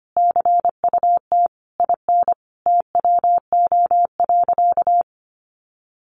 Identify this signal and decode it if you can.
Morse code, heard as a keyed tone, about 25 wpm, 700 Hz. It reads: CUT IN TWO.
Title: Sentences from Top 1300 Words 25wpm